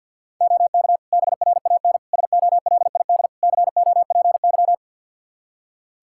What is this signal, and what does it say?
Morse code, heard as a keyed tone, about 51 wpm, 700 Hz. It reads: OK BRAN SOLID COPY
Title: Practice QSO - Exchange 3 - More Info - 50wpm